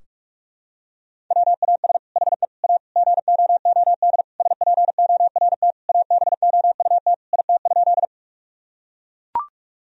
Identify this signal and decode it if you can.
Morse code, heard as a keyed tone, about 43 wpm, 700 Hz. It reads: WAS HE A GOOD SPORT ABOUT IT?
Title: Sentences from Top 1000 Words - Review Part 4 - 45wpm